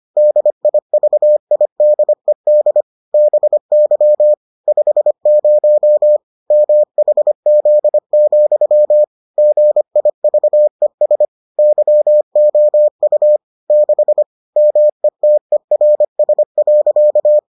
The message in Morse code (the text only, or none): DIVIDED BY 50 MHZ, GIVES YOU 6 METERS.